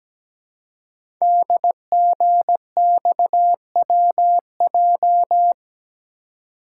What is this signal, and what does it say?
Morse code, heard as a keyed tone, about 17 wpm, 700 Hz. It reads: DGXWJ
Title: Code Groups - 5 Letters - 17wpm